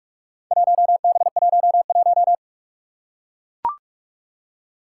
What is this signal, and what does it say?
Morse code, heard as a keyed tone, about 42 wpm, 700 Hz. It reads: 1B11
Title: Four Challenging Letter-Number Combinations 45wpm